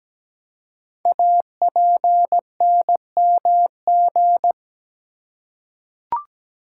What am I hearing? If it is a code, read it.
Morse code, heard as a keyed tone, about 17 wpm, 700 Hz. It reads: APNMG
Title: Code Groups - 5 Letters - 17wpm